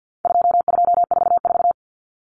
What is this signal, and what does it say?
Morse code, heard as a keyed tone, about 50 wpm, 700 Hz. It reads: 2244
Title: Common Cut Numbers - 4-Digit 50wpm